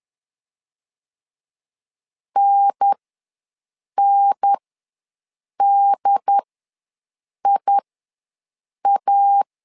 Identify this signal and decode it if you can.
Morse code, heard as a keyed tone, about 10 wpm, spaced as for 6 wpm, 800 Hz. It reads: NNDIA
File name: 2 UHDN ws 10wpm